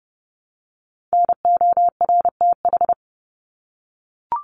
Morse code, heard as a keyed tone, about 30 wpm, 700 Hz. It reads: NORTH
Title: Top 300 Words Repeated 30wpm